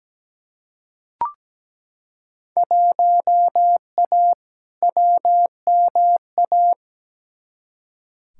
Morse code, heard as a keyed tone, about 17 wpm, 700 Hz. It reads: 1A WMA
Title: Field Day Exchanges - 17wpm